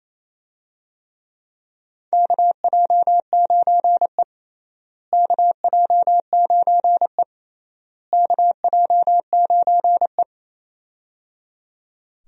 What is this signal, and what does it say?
Morse code, heard as a keyed tone, about 28 wpm, 700 Hz. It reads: KJ9E KJ9E KJ9E
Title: USA Call Signs - 4 Characters - Repeated 3x - 28wpm